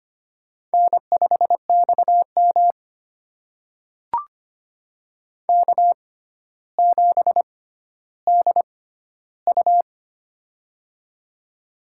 Morse code, heard as a keyed tone, about 25 wpm, spaced as for 13 wpm, 700 Hz. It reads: N5XM K7DU
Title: USA Call Signs - 6x Character Spacing - 25wpm